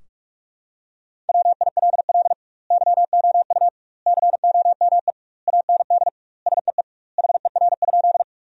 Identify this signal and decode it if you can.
Morse code, heard as a keyed tone, about 46 wpm, 700 Hz. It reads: WILL YOU COME AND SEE HER?
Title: 2x Word-Spacing - Sentences from Top 100 Words - 45wpm